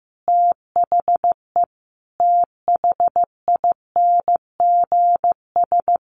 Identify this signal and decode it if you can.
Morse code, heard as a keyed tone, about 15 wpm, 700 Hz. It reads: THE THINGS
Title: Sets of 5 Words 15wpm